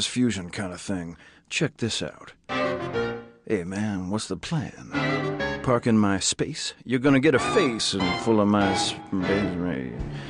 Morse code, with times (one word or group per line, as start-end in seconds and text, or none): none